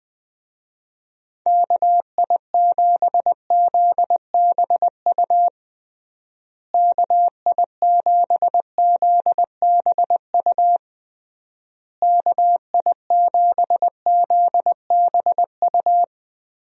Morse code, heard as a keyed tone, about 20 wpm, 700 Hz. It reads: KI7ZBU KI7ZBU KI7ZBU
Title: USA Call Signs - 6 Characters - Repeated 3x - 20wpm